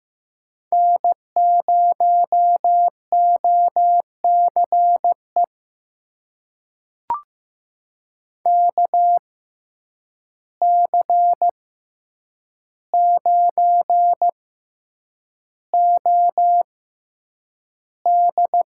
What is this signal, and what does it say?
Morse code, heard as a keyed tone, about 15 wpm, spaced as for 8 wpm, 700 Hz. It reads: N0OCE KC9OD
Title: USA Call Signs - 5 Characters - 6x Character Spacing - 15wpm